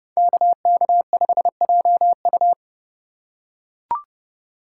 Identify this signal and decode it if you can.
Morse code, heard as a keyed tone, about 30 wpm, 700 Hz. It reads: KK5JU